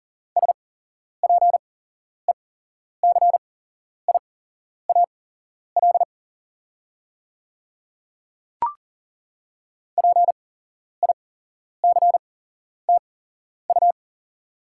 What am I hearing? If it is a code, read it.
Morse code, heard as a keyed tone, about 37 wpm, spaced as for 11 wpm, 690 Hz. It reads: SPECIAL PICTU
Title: 8x Character Spacing - Top 500 Words - 40wpm